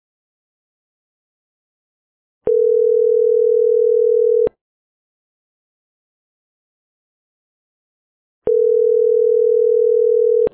Ringback tone (North America), 440 + 480 Hz.